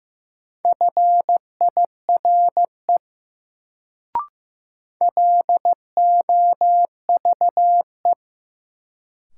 Morse code, heard as a keyed tone, about 15 wpm, 700 Hz. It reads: FIRE LOVE
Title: Sentences from Top 400 Words - New Word Drill - 15wpm